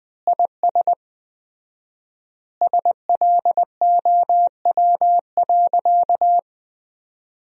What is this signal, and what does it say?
Morse code, heard as a keyed tone, about 20 wpm, 700 Hz. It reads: IS SLOW.
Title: 4x Word-Spacing - Sentences from Top 400 Words - 20wpm